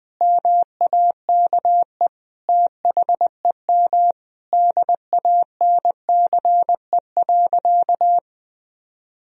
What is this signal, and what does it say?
Morse code, heard as a keyed tone, about 20 wpm, 700 Hz. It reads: MAKE THEM DANCE.
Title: Sentences from Top 1000 Words - Review Part 4 - 20wpm